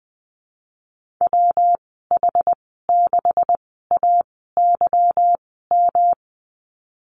Morse code, heard as a keyed tone, about 20 wpm, spaced as for 15 wpm, 700 Hz. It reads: WH6AYM
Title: 167 Lesson 27g - Number 7 - Call Signs 20wpm